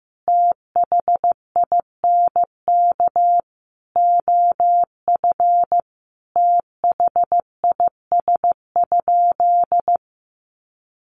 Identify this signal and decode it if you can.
Morse code, heard as a keyed tone, about 15 wpm, 700 Hz. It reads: THINK OF THIS?
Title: Sentences from Top 200 Words 15wpm